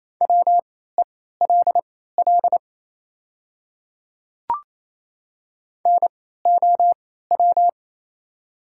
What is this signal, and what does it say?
Morse code, heard as a keyed tone, about 27 wpm, spaced as for 16 wpm, 700 Hz. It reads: WELL NOW
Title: 3x Character Spacing - Top 200 Words - 28wpm